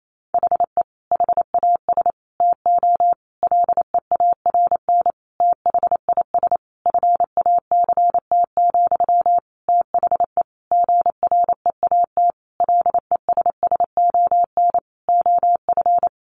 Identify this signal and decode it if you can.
Morse code, heard as a keyed tone, about 28 wpm, 700 Hz. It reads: HE HAS TO LEARN THIS FACT, THE GREAT LESSON OF
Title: Sentences from Top 1200 Words 28wpm